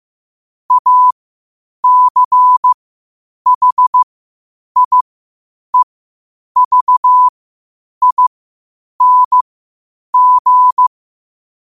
Morse code, heard as a keyed tone, about 15 wpm, spaced as for 8 wpm, 1000 Hz. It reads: ACHIEVING
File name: qotdfiles / 8wpm